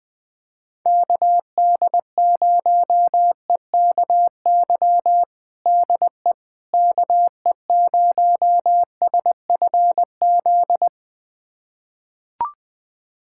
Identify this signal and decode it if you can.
Morse code, heard as a keyed tone, about 21 wpm, 700 Hz. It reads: KD0EKY DE KE0SFZ
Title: USA Callsign DE Callsign - 6 Characters - 20wpm